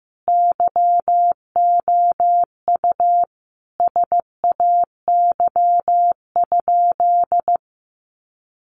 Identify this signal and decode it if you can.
Morse code, heard as a keyed tone, about 15 wpm, 700 Hz. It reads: YOU SAY?
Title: Sentences from Top 200 Words 15wpm